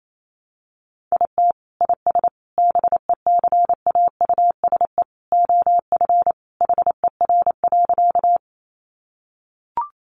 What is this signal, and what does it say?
Morse code, heard as a keyed tone, about 28 wpm, 700 Hz. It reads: IT IS BECAUSE OF HER.
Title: Sentences from Top 600 Words 28wpm